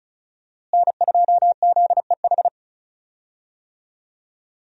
Morse code, heard as a keyed tone, about 34 wpm, 700 Hz. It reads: N2ZEH
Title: USA Call Signs - 5 Characters - Repeated 2x - 35wpm